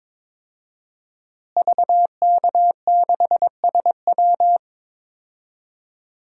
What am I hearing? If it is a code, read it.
Morse code, heard as a keyed tone, about 22 wpm, 700 Hz. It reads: VK6SW
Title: DXCC-100 Call Signs - 5 Characters - Repeated 3x - 22wpm